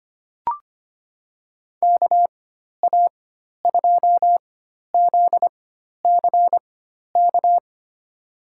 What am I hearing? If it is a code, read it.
Morse code, heard as a keyed tone, about 25 wpm, spaced as for 12 wpm, 700 Hz. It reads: KA2ZCK
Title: USA Call Signs - 4x Character Spacing - 25wpm